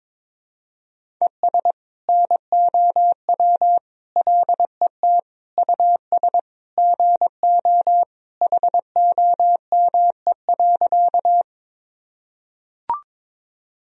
Morse code, heard as a keyed tone, about 22 wpm, 700 Hz. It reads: ES NOW LET US GO HOME.
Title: Sentences from Top 300 Words and Abbreviations 22wpm